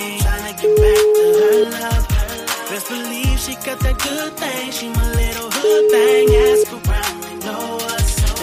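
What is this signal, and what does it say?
Ringback tone (Europe, 425 Hz), with noise.
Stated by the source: Caller ringback melody from the O2 Knihovna service of O2, a Czech mobile operator, played to callers before the call is answered.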